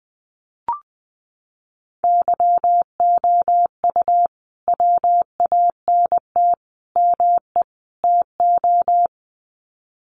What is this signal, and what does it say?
Morse code, heard as a keyed tone, about 20 wpm, 700 Hz. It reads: YOU WANT ME TO
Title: Sets of 4 Words 20wpm